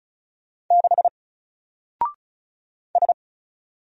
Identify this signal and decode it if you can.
Morse code, heard as a keyed tone, about 36 wpm, 700 Hz. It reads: 6 S
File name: Single-Letter-Number-35wpm